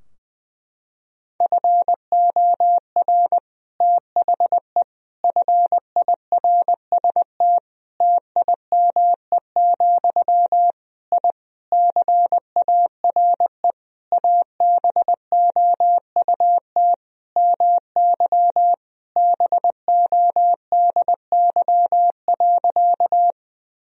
Morse code, heard as a keyed tone, about 20 wpm, 700 Hz. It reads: FOR THE FIRST TIME, I CARE ABOUT MY BODY.
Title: Sentences from Top 1000 Words - Review Part 2 - 20wpm